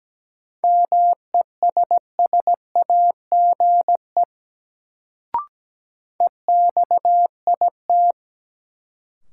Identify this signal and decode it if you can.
Morse code, heard as a keyed tone, about 17 wpm, 700 Hz. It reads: MESSAGE EXIT
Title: Sentences from Top 1100 Words - New Word Drill - 17wpm